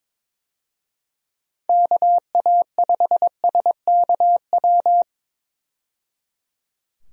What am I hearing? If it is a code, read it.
Morse code, heard as a keyed tone, about 22 wpm, 700 Hz. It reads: KA5SKW
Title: USA Call Signs - 6 Characters - 22wpm